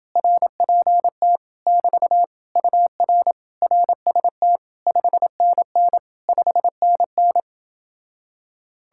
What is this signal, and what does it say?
Morse code, heard as a keyed tone, about 30 wpm, 700 Hz. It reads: RPT <BT> UR RST 5NN 5NN